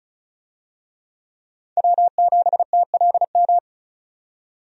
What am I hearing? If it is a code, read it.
Morse code, heard as a keyed tone, about 36 wpm, 700 Hz. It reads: W7TLM
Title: USA Call Signs - 5 Characters - 35wpm